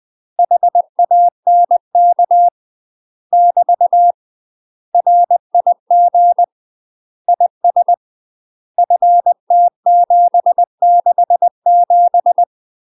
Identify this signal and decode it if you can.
Morse code, heard as a keyed tone, about 20 wpm, 700 Hz. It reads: HANK <BT> RIG IS FT767
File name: UK MASTER QSOs TRAINING / 20WPM / UK MASTER QSOs 0066